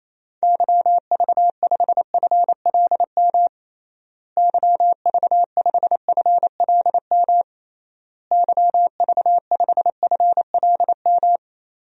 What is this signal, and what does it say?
Morse code, heard as a keyed tone, about 28 wpm, 700 Hz. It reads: YV5FLM YV5FLM YV5FLM
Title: DXCC-100 Call Signs - 6 Characters - Repeated 3x - 28wpm